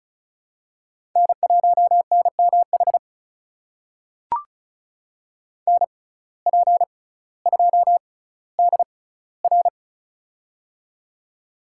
Morse code, heard as a keyed tone, about 34 wpm, spaced as for 18 wpm, 700 Hz. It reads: N1NMH NP2DR
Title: USA Call Signs - 5 Characters - 6x Character Spacing - 35wpm